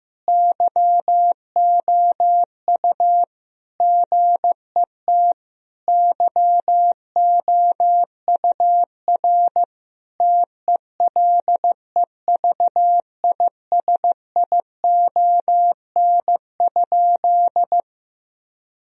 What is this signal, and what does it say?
Morse code, heard as a keyed tone, about 15 wpm, 700 Hz. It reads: YOU GET YOUR TELEVISION?
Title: Sentences from Top 1000 Words - Review Part 4 - 15wpm